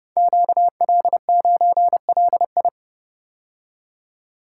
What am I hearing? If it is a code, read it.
Morse code, heard as a keyed tone, about 30 wpm, 700 Hz. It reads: QL9LI
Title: Code Groups - 5 Letter:Numbers - 30wpm